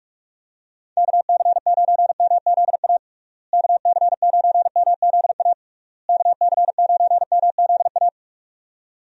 Morse code, heard as a keyed tone, about 46 wpm, 700 Hz. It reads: KC9MZA KC9MZA KC9MZA
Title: USA Call Signs - 6 Characters - Repeated 3x - 45wpm